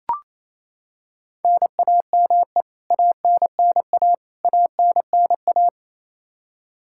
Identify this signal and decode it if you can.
Morse code, heard as a keyed tone, about 28 wpm, 700 Hz. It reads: NAME ANNA ANNA
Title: K1USN SST QSO Practice 28wpm